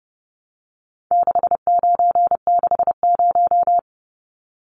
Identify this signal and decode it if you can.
Morse code, heard as a keyed tone, about 30 wpm, 700 Hz. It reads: 6960
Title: Numbers - 4-Digit 30wpm